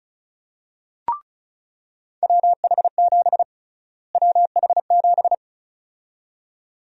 Morse code, heard as a keyed tone, about 34 wpm, 700 Hz. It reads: WH7 WH7
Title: USA Call Sign Prefixes - Repeated 2x - 35wpm